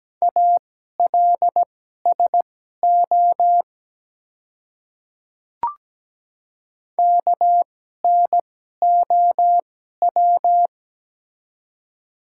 Morse code, heard as a keyed tone, about 17 wpm, spaced as for 12 wpm, 700 Hz. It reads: ALSO KNOW
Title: 2x Character Spacing - Top 200 Words - 17wpm